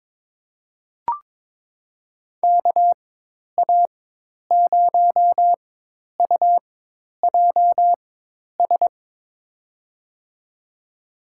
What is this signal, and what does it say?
Morse code, heard as a keyed tone, about 22 wpm, spaced as for 10 wpm, 700 Hz. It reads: KA0UJS
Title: USA Call Signs - 4x Character Spacing - 22wpm